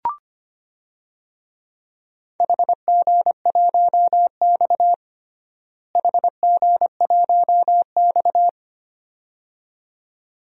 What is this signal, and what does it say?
Morse code, heard as a keyed tone, about 25 wpm, 700 Hz. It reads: HG1X HG1X
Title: DXCC-100 Call Signs - 4 Characters - Repeated 2x - 25wpm